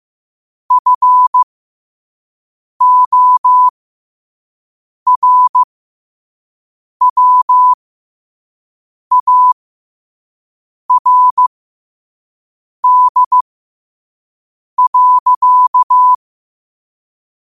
Morse code, heard as a keyed tone, about 15 wpm, spaced as for 5 wpm, 1000 Hz. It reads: FORWARD.
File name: qotdfiles / 5wpm